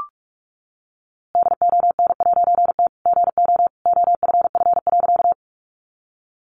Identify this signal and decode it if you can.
Morse code, heard as a keyed tone, about 44 wpm, 700 Hz. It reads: DON'T GO OFF.
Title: A Sound Of Thunder 45wpm